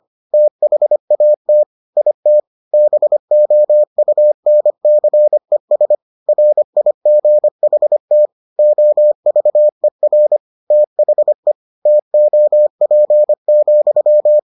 Morse code, heard as a keyed tone, about 25 wpm, 600 Hz. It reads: THAT IT BOUNCES RIGHT OVER THE TOP,